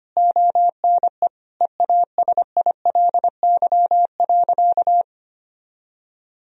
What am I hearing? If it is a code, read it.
Morse code, heard as a keyed tone, about 25 wpm, 700 Hz. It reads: ONE EASILY.
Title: Sentences from Top 1000 Words - Review Part 3 - 25wpm